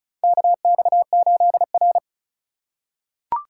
Morse code, heard as a keyed tone, about 34 wpm, 700 Hz. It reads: KX8R
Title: USA Call Signs - 4 Characters - 3x Character Spacing - 35wpm